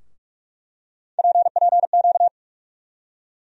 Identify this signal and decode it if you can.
Morse code, heard as a keyed tone, about 42 wpm, 700 Hz. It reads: PPQ
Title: USA Call Sign Postfixes - 3 Letters 45wpm